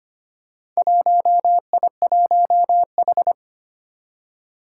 Morse code, heard as a keyed tone, about 25 wpm, 700 Hz. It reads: 1I1H
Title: Four Challenging Letter-Number Combinations 25wpm